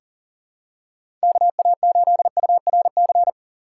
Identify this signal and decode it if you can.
Morse code, heard as a keyed tone, about 39 wpm, 700 Hz. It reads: KA8URC
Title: USA Call Signs - 6 Characters - 4x Character Spacing - 40wpm